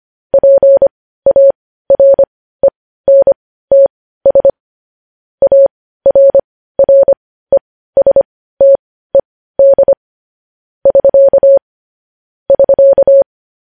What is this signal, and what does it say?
Morse code, heard as a keyed tone, about 25 wpm, spaced as for 15 wpm, 550 Hz. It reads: PARENTS ARRESTED <SK> <SK>
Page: news-headlines-in-morse-code-at-15-wpm